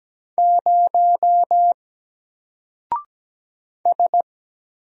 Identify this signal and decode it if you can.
Morse code, heard as a keyed tone, about 17 wpm, 700 Hz. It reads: 0 S
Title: Single Character (Letter:Number:Punctuation:Prosign) 17wpm